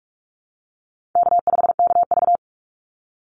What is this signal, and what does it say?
Morse code, heard as a keyed tone, about 44 wpm, 700 Hz. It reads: K5KV
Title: Call Signs Repeated 2x - 45wpm